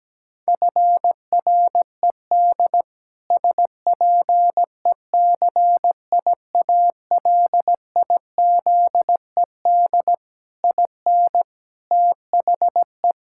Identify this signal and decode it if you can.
Morse code, heard as a keyed tone, about 17 wpm, 700 Hz. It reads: FRED SPECIALIZED IN THE
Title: Pangram Challenge v2 - 17wpm